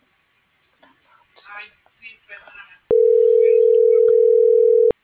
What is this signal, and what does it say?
Ringback tone (North America), 440 + 480 Hz.